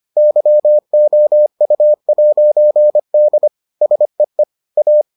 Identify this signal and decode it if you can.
Morse code, heard as a keyed tone, about 25 wpm, 600 Hz. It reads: YOU'D SEE A